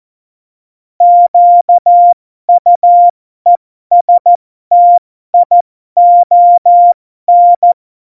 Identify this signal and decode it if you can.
Morse code, heard as a keyed tone, about 14 wpm, spaced as for 12 wpm, 700 Hz. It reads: QUESTION
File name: Morsekurs / Witze